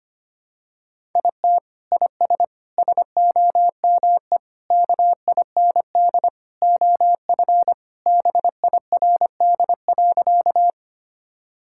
Morse code, heard as a keyed tone, about 25 wpm, 700 Hz. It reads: IT IS SOME KIND OF BIRD.
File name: practice / 25wpm / Sentences with 3-Word Phrases - 25wpm